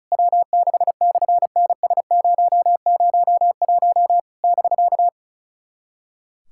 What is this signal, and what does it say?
Morse code, heard as a keyed tone, about 34 wpm, 700 Hz. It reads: W6/NS001 <BK>
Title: SOTA QSO Practice 35wpm